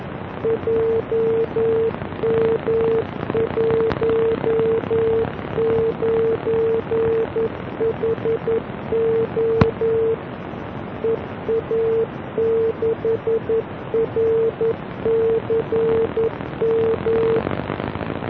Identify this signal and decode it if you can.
Morse code, heard as a keyed tone, about 11 wpm, 450 Hz. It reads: JM19HO EA6RCM